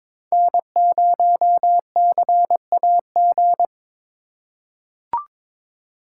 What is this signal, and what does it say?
Morse code, heard as a keyed tone, about 22 wpm, 700 Hz. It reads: N0CAG
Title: USA Call Signs - 5 Characters - 2x Character Spacing - 22wpm